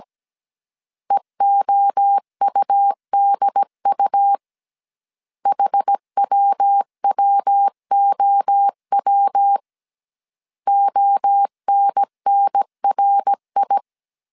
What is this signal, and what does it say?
Morse code, heard as a keyed tone, about 17 wpm, 800 Hz. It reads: EOUDU HWWOW ONNRI